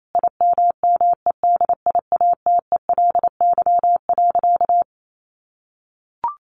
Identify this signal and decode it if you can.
Morse code, heard as a keyed tone, about 28 wpm, 700 Hz. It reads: IMMEDIATELY.
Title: Sentences from Top 1100 Words 28wpm